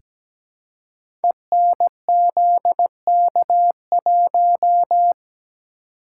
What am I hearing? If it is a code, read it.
Morse code, heard as a keyed tone, about 17 wpm, 700 Hz. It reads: ENZK1
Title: Code Groups - 5 Letter:Numbers - 17wpm